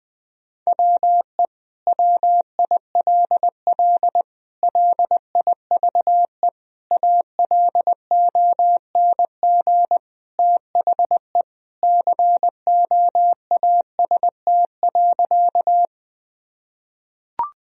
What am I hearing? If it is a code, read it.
Morse code, heard as a keyed tone, about 20 wpm, 700 Hz. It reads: WE WILL LIVE ALONG THE COAST.
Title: Sentences from Top 1000 Words - Review Part 4 - 20wpm